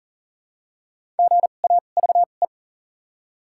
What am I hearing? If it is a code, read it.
Morse code, heard as a keyed tone, about 40 wpm, spaced as for 29 wpm, 700 Hz. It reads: GAVE
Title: 2x Character Spacing - Top 500 Words - 40wpm